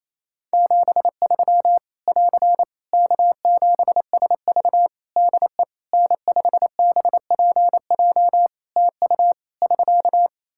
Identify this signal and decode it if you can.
Morse code, heard as a keyed tone, about 27 wpm, 700 Hz. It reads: 73 <AR> K7SV DE N5BPJ TU <SK>
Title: Practice QSO - Exchange 4 - Ending - 28wpm